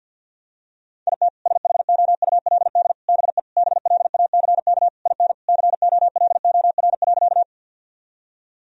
Morse code, heard as a keyed tone, about 48 wpm, 700 Hz. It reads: IT SHOULD BE BLACK IN COLOR.
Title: Sentences from Top 1000 Words - Review Part 2 - 50wpm